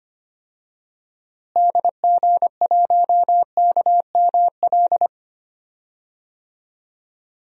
Morse code, heard as a keyed tone, about 25 wpm, 700 Hz. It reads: DG1KML
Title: DXCC-100 Call Signs - 6 Characters - Repeated 3x - 25wpm